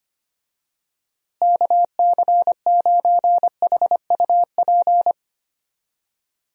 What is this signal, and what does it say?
Morse code, heard as a keyed tone, about 25 wpm, 700 Hz. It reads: KC9HUP
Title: USA Call Signs - 6 Characters - Repeated 3x - 25wpm